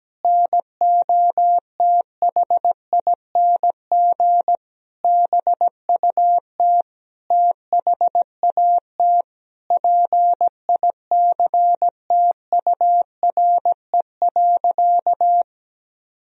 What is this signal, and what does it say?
Morse code, heard as a keyed tone, about 17 wpm, 700 Hz. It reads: NOTHING BUT THAT PICTURE.
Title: Sentences from Top 1000 Words - Review Part 1 - 17wpm